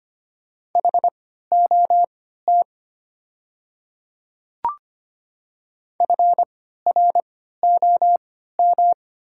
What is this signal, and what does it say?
Morse code, heard as a keyed tone, about 25 wpm, spaced as for 14 wpm, 700 Hz. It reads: HOT FROM
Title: 3x Character Spacing - Top 100 Words - 25wpm